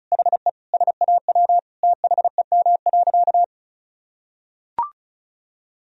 Morse code, heard as a keyed tone, about 36 wpm, 700 Hz. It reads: HE SAW THEM.
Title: Sentences from Top 1000 Words - Review Part 1 - 35wpm